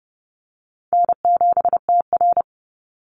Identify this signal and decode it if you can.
Morse code, heard as a keyed tone, about 30 wpm, 700 Hz. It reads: N7TR
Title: Easy Call Signs - 30wpm